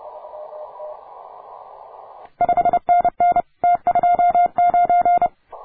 Morse code, heard as a keyed tone, about 29 wpm, 700 Hz. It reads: 5NN T29